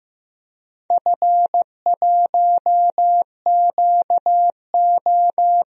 Morse code, heard as a keyed tone, about 15 wpm, 700 Hz. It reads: F1QO
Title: DXCC-100 Call Signs - 4 Characters - Repeated 2x - 15wpm